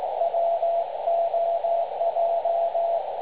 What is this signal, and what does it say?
Morse code, heard as a keyed tone, about 17 wpm, 680 Hz, with radio noise. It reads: WOJ
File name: dlazaawansowanych / nagraniazpasma / websdr recording 3565.0kHz